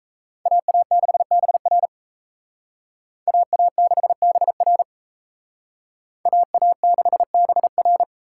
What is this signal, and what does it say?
Morse code, heard as a keyed tone, about 34 wpm, 700 Hz. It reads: AA6BR AA6BR AA6BR
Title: Speed Racing - USA 2x2 Call Signs 28wpm